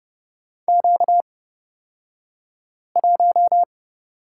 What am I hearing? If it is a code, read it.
Morse code, heard as a keyed tone, about 30 wpm, spaced as for 5 wpm, 700 Hz. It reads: Q1
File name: Lesson 38d - Letter q - Two Characters 30wpm